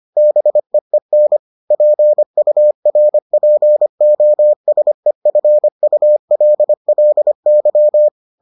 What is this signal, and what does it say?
Morse code, heard as a keyed tone, about 25 wpm, 600 Hz. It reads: BEEN PURPOSEFULLY